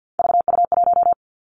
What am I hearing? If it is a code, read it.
Morse code, heard as a keyed tone, about 50 wpm, 700 Hz. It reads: VU1
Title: Three Challenging Letter-Number Combinations 50wpm